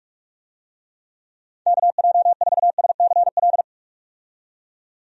Morse code, heard as a keyed tone, about 44 wpm, 700 Hz. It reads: KJ4SCL
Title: USA Call Signs - 6 Characters - 45wpm